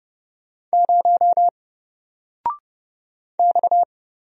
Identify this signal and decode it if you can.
Morse code, heard as a keyed tone, about 31 wpm, 700 Hz. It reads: 0 X
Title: Single Letter-Number Alt 30wpm